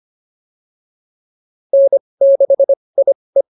Morse code, heard as a keyed tone, about 25 wpm, spaced as for 20 wpm, 590 Hz. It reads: N6IE